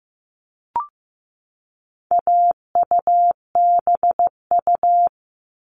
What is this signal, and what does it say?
Morse code, heard as a keyed tone, about 15 wpm, 700 Hz. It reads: AUBU
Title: All Cut Numbers - 4-Digit 15wpm